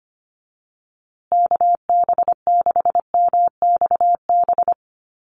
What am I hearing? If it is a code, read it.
Morse code, heard as a keyed tone, about 25 wpm, 700 Hz. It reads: KB6MXB
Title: USA Call Signs Repeated 2x 25wpm